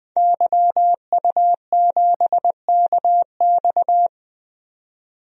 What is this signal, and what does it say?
Morse code, heard as a keyed tone, about 20 wpm, 700 Hz. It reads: YU7KX
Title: DXCC-100 Call Signs - 5 Characters - 20wpm